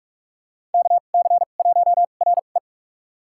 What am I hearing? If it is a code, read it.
Morse code, heard as a keyed tone, about 46 wpm, spaced as for 32 wpm, 700 Hz. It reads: KC1RE
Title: USA Call Signs - 5 Characters - 2x Character Spacing - 45wpm